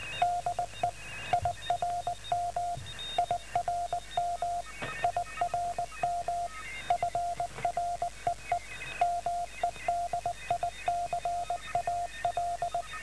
Morse code, heard as a keyed tone, about 19 wpm, 700 Hz, with radio noise. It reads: DE IRM IRM IRM FREE MEDICAL